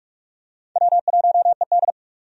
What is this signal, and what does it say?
Morse code, heard as a keyed tone, about 45 wpm, 700 Hz. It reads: W1ED
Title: USA Call Signs - 4 Characters - 45wpm